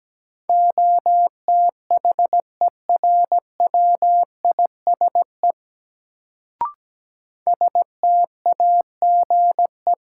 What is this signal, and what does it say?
Morse code, heard as a keyed tone, about 17 wpm, 700 Hz. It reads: OTHERWISE STAGE
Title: Sentences from Top 1200 Words - New Word Drill Alt - 17wpm